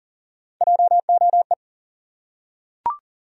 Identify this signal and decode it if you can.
Morse code, heard as a keyed tone, about 40 wpm, 700 Hz. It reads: JOE
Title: USA Common Names - Max 3 Letters - 40wpm